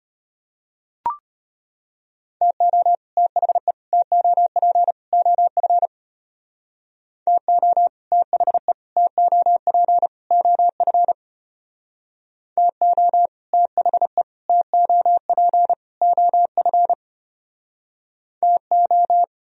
Morse code, heard as a keyed tone, about 30 wpm, 700 Hz. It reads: TO THE TOP OF TO THE TOP OF TO THE TOP OF TO
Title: Speed-Racing - Sets of 4 Words 25wpm